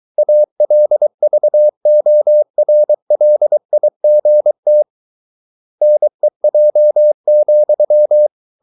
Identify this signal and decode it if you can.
Morse code, heard as a keyed tone, about 23 wpm, 600 Hz. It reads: ALVORLIGT NEJ,
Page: Ferie 23 wpm